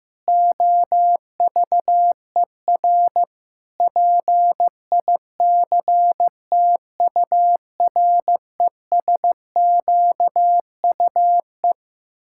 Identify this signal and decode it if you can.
Morse code, heard as a keyed tone, about 15 wpm, 700 Hz. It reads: OVER PICTURESQUE
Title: Pangram Challenge v2 - 15wpm